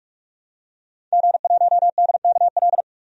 Morse code, heard as a keyed tone, about 46 wpm, 700 Hz. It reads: G1DKL
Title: DXCC-100 Call Signs - 5 Characters - 45wpm